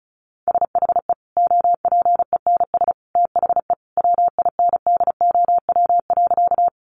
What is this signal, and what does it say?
Morse code, heard as a keyed tone, about 34 wpm, 700 Hz. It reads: SHE OPENS THE WINDOW.
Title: Easy Phrases 35wpm